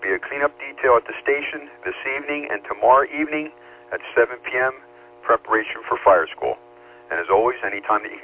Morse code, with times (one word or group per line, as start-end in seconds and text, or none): none